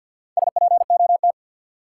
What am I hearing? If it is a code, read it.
Morse code, heard as a keyed tone, about 46 wpm, 700 Hz. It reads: SPOT
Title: Sentences from Top 1100 Words - New Word Drill Alt - 50wpm